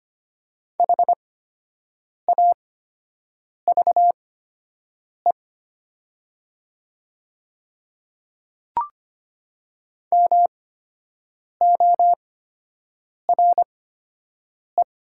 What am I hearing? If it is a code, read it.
Morse code, heard as a keyed tone, about 25 wpm, spaced as for 7 wpm, 700 Hz. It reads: HAVE MORE
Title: 8x Character Spacing - Top 100 Words - 25wpm